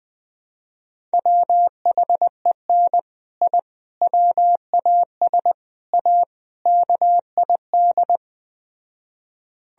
Morse code, heard as a keyed tone, about 20 wpm, 700 Hz. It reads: WHEN I WAS A KID
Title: Speed-Racing - Sets of 5 Words 17wpm